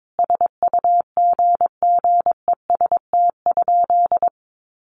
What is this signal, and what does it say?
Morse code, heard as a keyed tone, about 22 wpm, 700 Hz. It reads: SUGGEST?
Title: Sentences from Top 1300 Words 22wpm